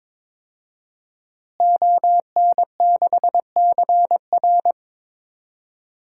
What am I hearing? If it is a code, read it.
Morse code, heard as a keyed tone, about 22 wpm, 700 Hz. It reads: ON6CR
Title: DXCC-100 Call Signs - 5 Characters - Repeated 2x - 22wpm